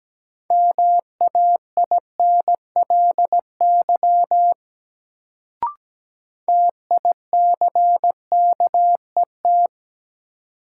Morse code, heard as a keyed tone, about 17 wpm, 700 Hz. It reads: MAINLY TICKET
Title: Sentences from Top 1400 Words - New Word Drill - 17wpm